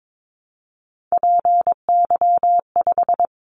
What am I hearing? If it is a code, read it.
Morse code, heard as a keyed tone, about 22 wpm, 700 Hz. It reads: PY5
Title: Callsign Prefixes for Top 100 DXCC - 22wpm